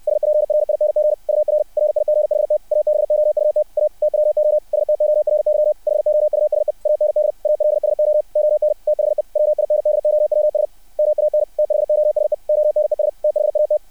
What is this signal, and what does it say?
Morse code, heard as a keyed tone, about 60 wpm, 600 Hz. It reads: SPRING UR RIGHT ABOUT A TWO STORY HOUSE AND ALSO ON THE WINDOWS MAN THOSE ONES THAT
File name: archived / CRSnet